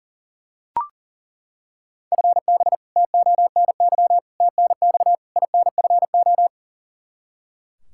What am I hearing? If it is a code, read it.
Morse code, heard as a keyed tone, about 40 wpm, 690 Hz. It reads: FB TONY TNX INFO
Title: Practice QSO - Exchange 3 - More Info - 40wpm